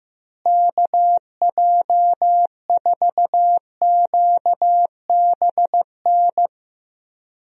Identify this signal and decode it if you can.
Morse code, heard as a keyed tone, about 15 wpm, 700 Hz. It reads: KJ4QBN
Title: USA Call Signs - 6 Characters - Repeated 2x - 15wpm